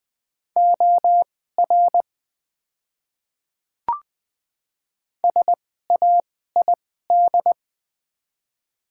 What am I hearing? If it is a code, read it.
Morse code, heard as a keyed tone, about 20 wpm, spaced as for 14 wpm, 700 Hz. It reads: OR SAID
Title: 2x Character Spacing - Top 300 Words - 20wpm